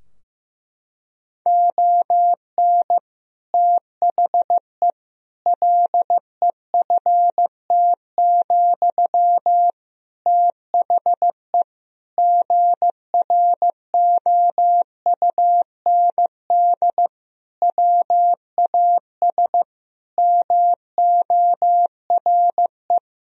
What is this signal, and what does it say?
Morse code, heard as a keyed tone, about 15 wpm, 700 Hz. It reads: ON THE LEFT, THE GROUND WAS MORE
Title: Sentences from Top 1000 Words - Review Part 4 - 15wpm